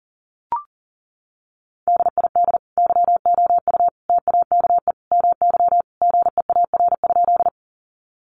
Morse code, heard as a keyed tone, about 38 wpm, 700 Hz. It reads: DID YOU TAKE MY GEAR?
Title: Sentences from Top 1400 Words 40wpm